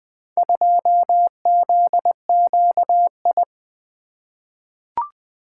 Morse code, heard as a keyed tone, about 20 wpm, 700 Hz. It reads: 2ZQI
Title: Four Letter-Number Combinations - 20wpm